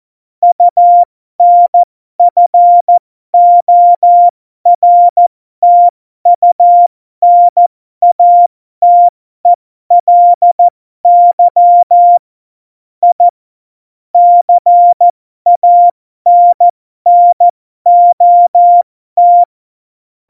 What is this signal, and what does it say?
Morse code, heard as a keyed tone, about 14 wpm, spaced as for 12 wpm, 700 Hz. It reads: UNFORTUNATELY I CANNOT
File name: Morsekurs / Witze